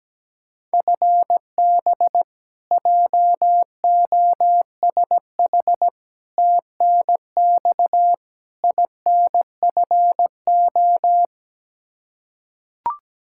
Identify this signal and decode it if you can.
Morse code, heard as a keyed tone, about 17 wpm, 700 Hz. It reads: FB JOSH TNX INFO
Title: Practice QSO - Exchange 3 - More Info - 17wpm